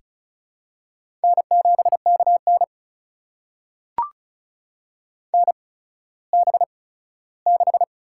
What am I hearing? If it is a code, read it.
Morse code, heard as a keyed tone, about 34 wpm, spaced as for 17 wpm, 700 Hz. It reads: N7KN NB6
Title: USA Call Signs - 4 Characters - 8x Character Spacing - 35wpm